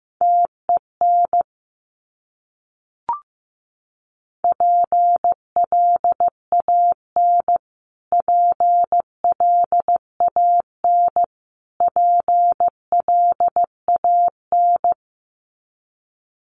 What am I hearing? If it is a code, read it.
Morse code, heard as a keyed tone, about 15 wpm, 700 Hz. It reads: TEN PLAN PLAN PLAN
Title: Top 500 Words Repeated 15wpm